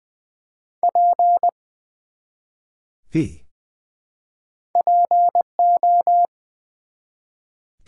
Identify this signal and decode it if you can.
Morse code, heard as a keyed tone, about 20 wpm, 700 Hz. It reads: P PO
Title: Build A Word - 1 Letter At A Time - 5-Letter Words - 20wpm